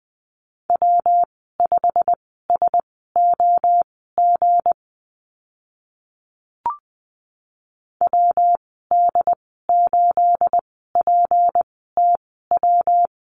Morse code, heard as a keyed tone, about 20 wpm, spaced as for 14 wpm, 700 Hz. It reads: W5SOG WD8PTW
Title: 213 Lesson 34g - Number 8 - Call Signs 20wpm